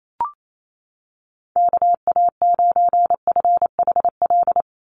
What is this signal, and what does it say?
Morse code, heard as a keyed tone, about 28 wpm, 700 Hz. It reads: KA9FHL
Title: USA Call Signs 28wpm